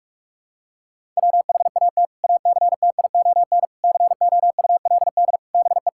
Morse code, heard as a keyed tone, about 41 wpm, 700 Hz. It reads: WHAT ACTION COULD BE
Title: Sentences from Top 1000 Words - Review Part 6 - 45wpm